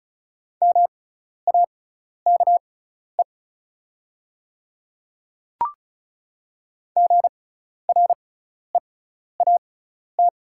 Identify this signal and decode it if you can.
Morse code, heard as a keyed tone, about 34 wpm, spaced as for 12 wpm, 700 Hz. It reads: MAKE GREAT
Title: 6x Character Spacing - Top 300 Words - 35wpm